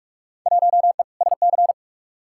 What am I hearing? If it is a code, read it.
Morse code, heard as a keyed tone, about 47 wpm, 700 Hz. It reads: 1E SC
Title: Field Day Exchanges - 45wpm